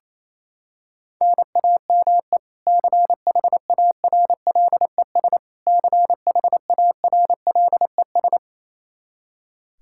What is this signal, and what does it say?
Morse code, heard as a keyed tone, about 27 wpm, 700 Hz. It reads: NAME CHARLES CHARLES
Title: Practice QSO - Exchange 1 - Beginning - 28wpm